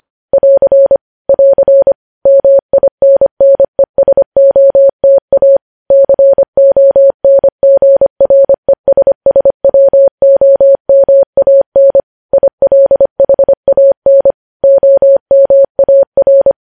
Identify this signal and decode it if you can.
Morse code, heard as a keyed tone, about 25 wpm, 550 Hz. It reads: <AR> <AR> MINNESOTA CONGRESSWOMAN ILHAN OMAR
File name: podcast / News headlines-1x-25x25